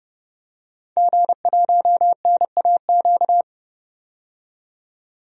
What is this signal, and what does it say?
Morse code, heard as a keyed tone, about 30 wpm, 700 Hz. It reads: G1NAQ
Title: DXCC-100 Call Signs - 5 Characters - 30wpm